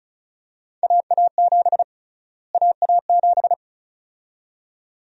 Morse code, heard as a keyed tone, about 36 wpm, 700 Hz. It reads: AA7 AA7
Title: USA Call Sign Prefixes - Repeated 2x - 35wpm